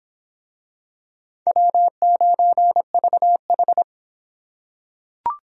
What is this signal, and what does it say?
Morse code, heard as a keyed tone, about 24 wpm, 700 Hz. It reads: W9VH